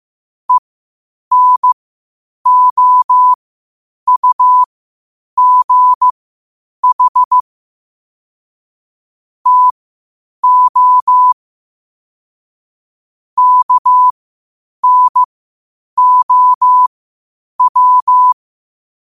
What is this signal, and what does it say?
Morse code, heard as a keyed tone, about 15 wpm, spaced as for 8 wpm, 1000 Hz. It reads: ENOUGH TO KNOW
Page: qotd-at-8-wpm